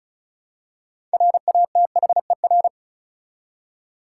Morse code, heard as a keyed tone, about 34 wpm, 700 Hz. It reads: RATHER
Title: Six Letter Words - 35wpm